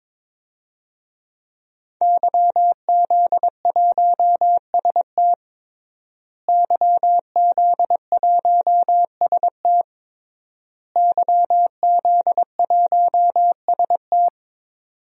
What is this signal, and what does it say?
Morse code, heard as a keyed tone, about 22 wpm, 700 Hz. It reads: YZ1ST YZ1ST YZ1ST
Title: DXCC-100 Call Signs - 5 Characters - Repeated 3x - 22wpm